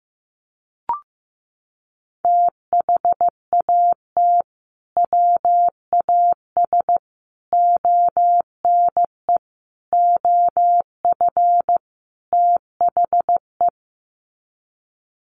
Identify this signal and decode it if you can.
Morse code, heard as a keyed tone, about 15 wpm, 700 Hz. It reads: THAT WAS ONE OF THE
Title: Sets of 5 Words 15wpm